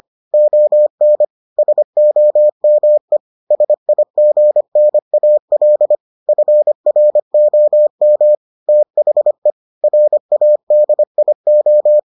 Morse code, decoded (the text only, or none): ON SOME SIGNAL FROM THE RADIO